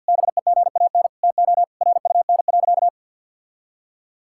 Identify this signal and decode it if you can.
Morse code, heard as a keyed tone, about 51 wpm, 700 Hz. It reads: BEGAN TO RUN.
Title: Sentences from Top 1000 Words - Review Part 5 - 50wpm